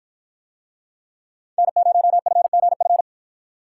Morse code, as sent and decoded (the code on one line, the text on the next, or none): -. ----- ..-. --. ..-.
N0FGF